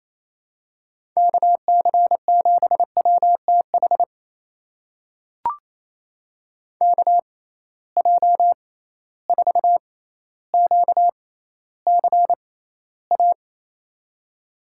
Morse code, heard as a keyed tone, about 28 wpm, spaced as for 14 wpm, 700 Hz. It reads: KC7WTH KJ4QCA
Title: USA Call Signs - 6 Characters - 6x Character Spacing - 28wpm